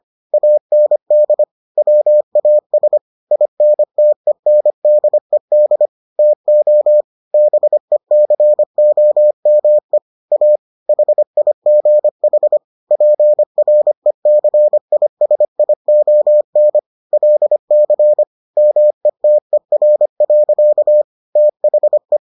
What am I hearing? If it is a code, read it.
Morse code, heard as a keyed tone, about 25 wpm, 600 Hz. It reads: AND WAS INTENDED TO BECOME A HIGH PRECISION LC METER. THE